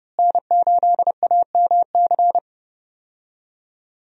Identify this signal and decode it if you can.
Morse code, heard as a keyed tone, about 30 wpm, 700 Hz. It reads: N8AMC
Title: USA Call Signs - 5 Characters - Repeated 3x - 30wpm